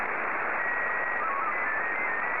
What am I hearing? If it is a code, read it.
SSTV (slow-scan television), Scottie 1.